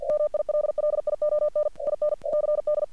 Morse code, heard as a keyed tone, about 46 wpm, 600 Hz. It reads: MILLION IN CD
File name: archived / CRSnet